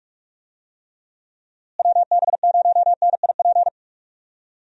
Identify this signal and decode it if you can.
Morse code, heard as a keyed tone, about 44 wpm, 700 Hz. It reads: WB0NIP
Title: USA Call Signs - 6 Characters - Repeated 2x - 45wpm